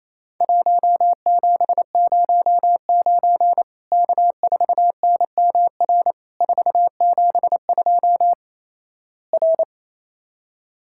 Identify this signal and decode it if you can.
Morse code, heard as a keyed tone, about 28 wpm, 700 Hz. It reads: 1709 K4NMR 472 R
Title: Worked All Europe DX Contest - QTC Exchange - 28wpm